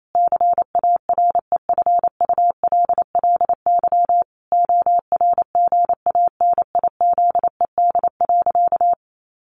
Morse code, heard as a keyed tone, about 28 wpm, 700 Hz. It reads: CAREFULLY ORGANIZED.
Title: Sentences from Top 1300 Words 28wpm